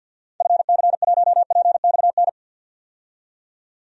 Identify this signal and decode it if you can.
Morse code, heard as a keyed tone, about 48 wpm, 700 Hz. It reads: FC1PXN
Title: DXCC-100 Call Signs - 6 Characters - 50wpm